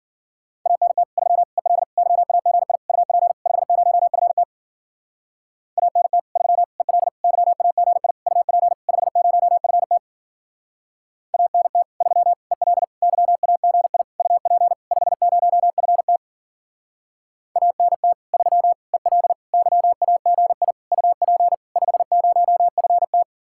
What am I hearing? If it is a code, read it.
Morse code, heard as a keyed tone, about 51 wpm, 700 Hz. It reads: ANT 3 EL YAGI UP 50FT ANT 3 EL YAGI UP 50FT ANT 3 EL YAGI UP 50FT ANT 3 EL YAGI UP 50FT